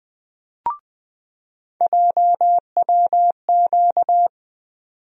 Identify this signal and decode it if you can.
Morse code, heard as a keyed tone, about 20 wpm, 700 Hz. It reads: JWQ
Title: USA Call Sign Postfixes - 3 Letters 20wpm